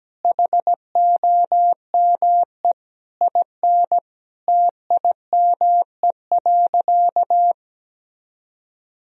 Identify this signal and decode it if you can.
Morse code, heard as a keyed tone, about 17 wpm, 700 Hz. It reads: HOME IN TIME.
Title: Sentences from Top 1000 Words - Review Part 3 - 17wpm